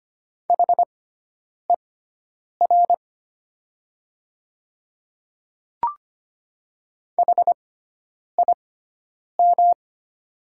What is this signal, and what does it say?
Morse code, heard as a keyed tone, about 25 wpm, spaced as for 9 wpm, 700 Hz. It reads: HER HIM
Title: 6x Character Spacing - Top 200 Words - 25wpm